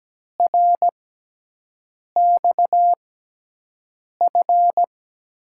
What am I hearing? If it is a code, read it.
Morse code, heard as a keyed tone, about 17 wpm, spaced as for 6 wpm, 700 Hz. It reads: RXF
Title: USA Call Signs - 6 Characters - 6x Character Spacing - 17wpm